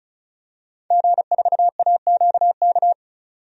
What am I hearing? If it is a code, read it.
Morse code, heard as a keyed tone, about 36 wpm, 700 Hz. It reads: G4AQK
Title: DXCC-100 Call Signs - 5 Characters - 35wpm